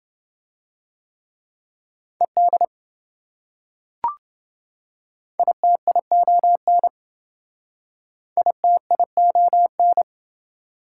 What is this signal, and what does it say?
Morse code, heard as a keyed tone, about 26 wpm, 700 Hz. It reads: ED ITION ITION